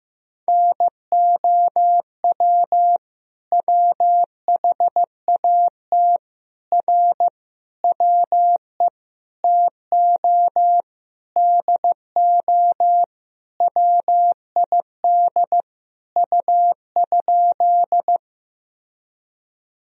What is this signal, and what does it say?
Morse code, heard as a keyed tone, about 15 wpm, 700 Hz. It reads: NOW WHAT R WE TO DO WID U?
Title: Sentences from Top 100 Words and Abbreviations 15wpm